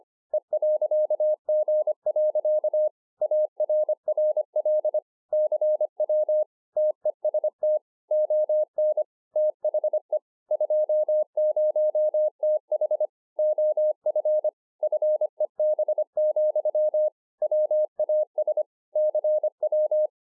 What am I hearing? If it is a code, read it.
Morse code, heard as a keyed tone, about 25 wpm, 610 Hz. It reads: E.G. ARRL CW TEST ON THE 20TH OF FEB, WAS CW